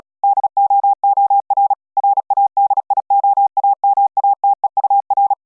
Morse code, heard as a keyed tone, about 40 wpm, 800 Hz. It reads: DOOR RADIOAMATEUR